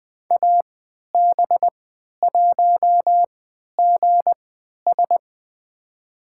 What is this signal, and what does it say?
Morse code, heard as a keyed tone, about 20 wpm, spaced as for 11 wpm, 700 Hz. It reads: AB1GS
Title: USA Call Signs - 5 Characters - 3x Character Spacing - 20wpm